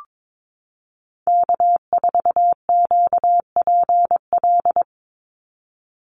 Morse code, heard as a keyed tone, about 22 wpm, 700 Hz. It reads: K4QPL
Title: Call Signs Repeated 2x - 22wpm